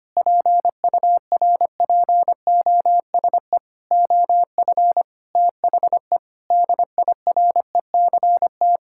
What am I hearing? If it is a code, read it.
Morse code, heard as a keyed tone, about 25 wpm, 700 Hz. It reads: PURPOSE OF THE DIRECT
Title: Sentences from Top 1000 Words - Review Part 5 - 25wpm